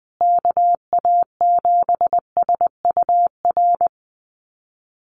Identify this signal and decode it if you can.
Morse code, heard as a keyed tone, about 20 wpm, 700 Hz. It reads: KA7SUR
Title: USA Call Signs Repeated 2x 20wpm